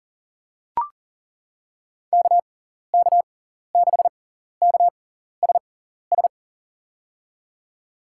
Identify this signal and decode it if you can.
Morse code, heard as a keyed tone, about 40 wpm, spaced as for 14 wpm, 700 Hz. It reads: KK6KSS
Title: USA Call Signs - 6 Characters - 6x Character Spacing - 40wpm